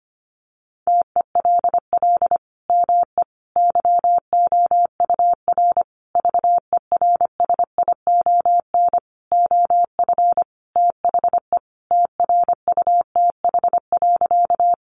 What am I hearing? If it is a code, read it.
Morse code, heard as a keyed tone, about 25 wpm, 700 Hz. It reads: TELL ME YOUR VERSION OF THE TRUTH.
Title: Sentences from Top 1400 Words 25wpm